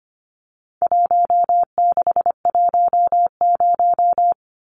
Morse code, heard as a keyed tone, about 25 wpm, 700 Hz. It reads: 1610
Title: Numbers - 4-Digit 25wpm